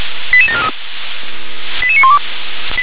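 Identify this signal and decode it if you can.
Five-tone selective call (ZVEI), heard as quick runs of five tones, tones of about 70 ms. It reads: 88743 88912
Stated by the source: ZVEI five-tone selective call 88743 received on an SDR.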